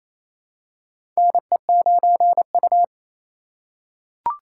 Morse code, heard as a keyed tone, about 27 wpm, 700 Hz. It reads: NE9U
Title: K1USN SST QSO Practice - 28wpm